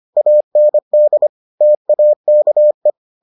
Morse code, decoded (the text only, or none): AND TAKE